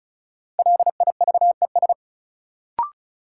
Morse code, heard as a keyed tone, about 36 wpm, 700 Hz. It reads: LIVES
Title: Sets of 5 Words Encore - New Word Drill - 35wpm